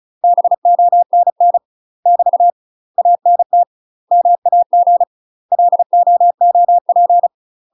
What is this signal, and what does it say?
Morse code, heard as a keyed tone, about 36 wpm, 700 Hz. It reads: BONN <BT> ANT MAG LOOP
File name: UK MASTER QSOs TRAINING / 35WPM / UK MASTER QSOs 0038